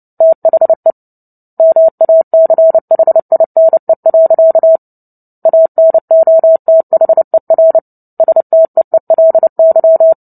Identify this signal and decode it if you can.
Morse code, heard as a keyed tone, about 29 wpm, 650 Hz. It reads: THE MACHINE. ANOTHER STEELY
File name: Book Two / 03-Chapter 3 - 29 WPM